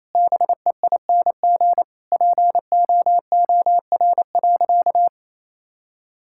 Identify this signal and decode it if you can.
Morse code, heard as a keyed tone, about 28 wpm, 700 Hz. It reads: BEING POOR.
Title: Sentences from Top 1000 Words - Review Part 6 - 28wpm